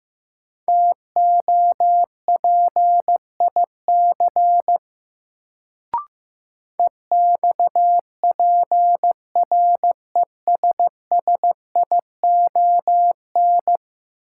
Morse code, heard as a keyed tone, about 16 wpm, 700 Hz. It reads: TOPIC EXPRESSION
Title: Sentences from Top 1200 Words - New Word Drill - 15wpm